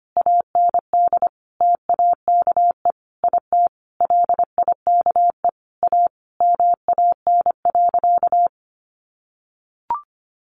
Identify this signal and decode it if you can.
Morse code, heard as a keyed tone, about 25 wpm, 700 Hz. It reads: AND TAKE IT LIKE A MAN.
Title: Sentences from Top 300 Words 25wpm